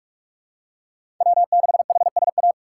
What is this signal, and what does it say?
Morse code, heard as a keyed tone, about 45 wpm, 700 Hz. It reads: W 6 H S A
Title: USA Call Signs - 5 Characters - 6x Character Spacing - 45wpm